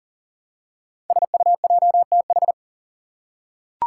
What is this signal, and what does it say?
Morse code, heard as a keyed tone, about 39 wpm, 700 Hz. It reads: SUJTH
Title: Code Groups - 5 Letter:Numbers - 40wpm